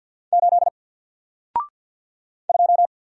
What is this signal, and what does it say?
Morse code, heard as a keyed tone, about 48 wpm, 700 Hz. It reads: 8 2
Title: Single Letter-Number Alt 50wpm